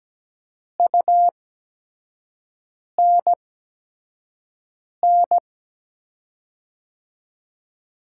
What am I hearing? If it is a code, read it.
Morse code, heard as a keyed tone, about 17 wpm, spaced as for 5 wpm, 700 Hz. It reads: UNN
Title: USA Call Signs - 5 Characters - 8x Character Spacing - 17wpm